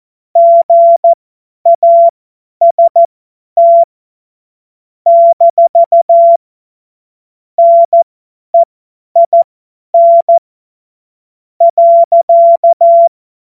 Morse code, heard as a keyed tone, about 14 wpm, spaced as for 10 wpm, 680 Hz. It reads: GAST - NEIN .
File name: Morsekurs / Witze / Witz 06. April 2021, 50 F70